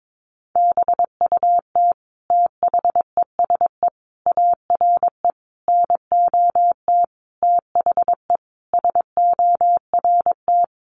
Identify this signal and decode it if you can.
Morse code, heard as a keyed tone, about 22 wpm, 700 Hz. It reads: BUT THESE ARE NOT THE SORT
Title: Sentences from Top 1300 Words 22wpm